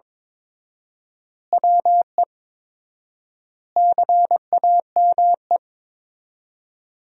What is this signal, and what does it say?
Morse code, heard as a keyed tone, about 21 wpm, 700 Hz. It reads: WE CAME
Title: 4x Word-Spacing - Sentences from Top 400 Words - 22wpm